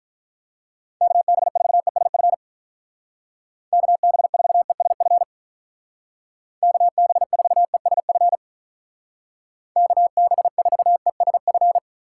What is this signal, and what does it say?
Morse code, heard as a keyed tone, about 44 wpm, 700 Hz. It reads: KB4ESF KB4ESF KB4ESF KB4ESF
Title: Speed Racing - USA 2x3 Call Signs 35wpm